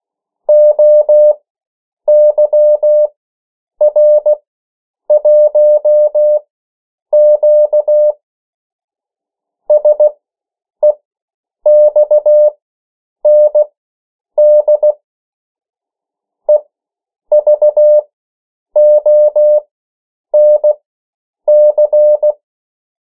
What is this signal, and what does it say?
Morse code, heard as a keyed tone, about 16 wpm, spaced as for 8 wpm, 600 Hz. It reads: OYR1Q SEXND EVONC
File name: morse mp3 / 14 urok 13